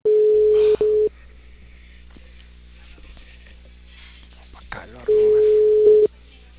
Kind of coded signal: telephone call progress tone